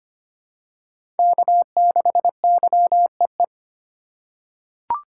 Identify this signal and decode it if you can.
Morse code, heard as a keyed tone, about 25 wpm, 700 Hz. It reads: K6YEE
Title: USA Call Signs - 5 Characters - 2x Character Spacing - 25wpm